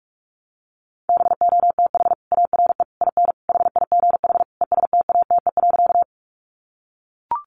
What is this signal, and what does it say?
Morse code, heard as a keyed tone, about 44 wpm, 700 Hz. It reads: BOTH ARE IN HIGH ESTATE.
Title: Sentences from Top 1200 Words 45wpm